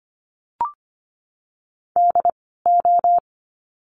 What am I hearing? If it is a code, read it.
Morse code, heard as a keyed tone, about 25 wpm, spaced as for 16 wpm, 700 Hz. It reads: DO
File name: course / 25:10wpm / 094 Lesson 16f - Letter c - Words 25wpm